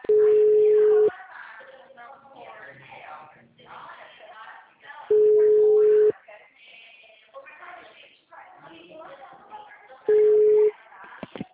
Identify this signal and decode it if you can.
Ringback tone (Europe, 425 Hz), with speech.
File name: out- -2005- - - .539